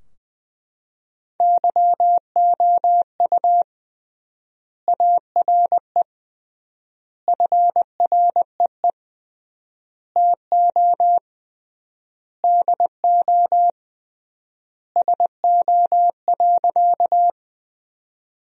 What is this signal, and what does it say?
Morse code, heard as a keyed tone, about 20 wpm, 700 Hz. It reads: YOU ARE FREE TO DO SO.
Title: 3x Word-Spacing - Sentences from Top 500 Words - 20wpm